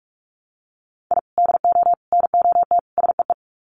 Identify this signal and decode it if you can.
Morse code, heard as a keyed tone, about 44 wpm, 700 Hz. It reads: I DO NOT SEE
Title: Sets of 4 Words 45wpm